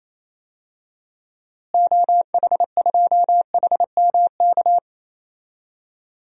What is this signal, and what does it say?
Morse code, heard as a keyed tone, about 28 wpm, 700 Hz. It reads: OH2HMK
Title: DXCC-100 Call Signs - 6 Characters - Repeated 3x - 28wpm